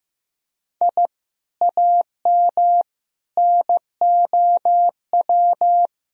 Morse code, heard as a keyed tone, about 16 wpm, 700 Hz. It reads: I AM NOW
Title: Sentences from Top 1000 Words - Review Part 3 - 15wpm